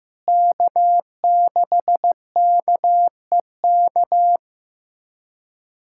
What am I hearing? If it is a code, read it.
Morse code, heard as a keyed tone, about 15 wpm, 700 Hz. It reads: K6KEK
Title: USA Callsign DE Callsign - 5 Characters - 15wpm